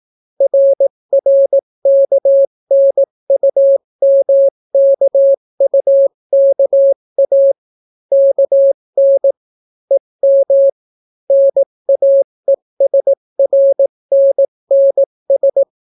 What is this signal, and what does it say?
Morse code, heard as a keyed tone, about 18 wpm, 550 Hz. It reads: RRKNUMKUKA KN EM NAESRNNS